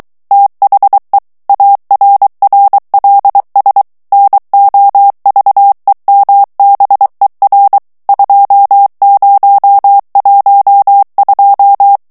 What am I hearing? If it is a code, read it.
Morse code, heard as a keyed tone, about 23 wpm, 800 Hz. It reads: THE ARRLS NOVEMBER 2012
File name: ANGLAIS texte clair 3 vitesse 22 mots minute